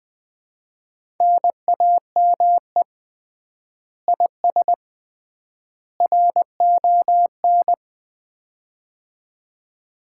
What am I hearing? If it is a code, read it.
Morse code, heard as a keyed tone, about 20 wpm, 700 Hz. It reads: NAME IS RON
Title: Common QSO Phrases - 3x Word-Spacing - 20wpm